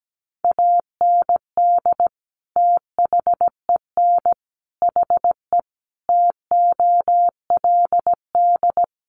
Morse code, heard as a keyed tone, about 17 wpm, 700 Hz. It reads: AND THEN HE TOLD
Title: Sentences from Top 400 Words 17wpm